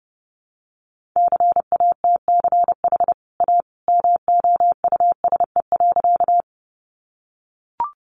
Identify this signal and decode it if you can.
Morse code, heard as a keyed tone, about 30 wpm, 700 Hz. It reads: CATCH A MOUSE.
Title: Sentences from Top 1500 Words 30wpm